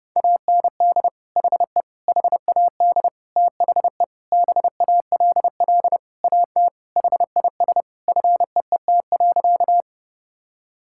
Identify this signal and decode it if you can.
Morse code, heard as a keyed tone, about 28 wpm, 700 Hz. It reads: AND HE HAD THE BALL AT HIS FEET.
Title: Sentences from Top 1000 Words - Review Part 6 - 30wpm